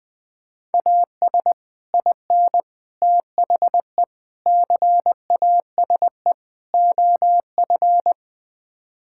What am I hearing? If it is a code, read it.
Morse code, heard as a keyed tone, about 20 wpm, 700 Hz. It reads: AS IN THE CASE OF
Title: Build A Phrase - 5-Word Phrases - 20wpm